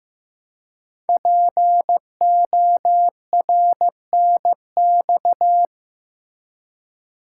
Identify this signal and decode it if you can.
Morse code, heard as a keyed tone, about 15 wpm, 700 Hz. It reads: PORNX
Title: Code Groups - 5 Letters - 15wpm